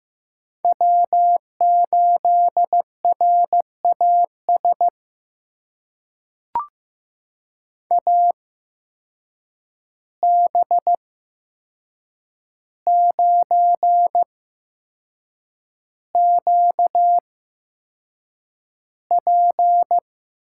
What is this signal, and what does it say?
Morse code, heard as a keyed tone, about 15 wpm, spaced as for 6 wpm, 700 Hz. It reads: W8RAS AB9QP
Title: USA Call Signs - 5 Characters - 8x Character Spacing - 15wpm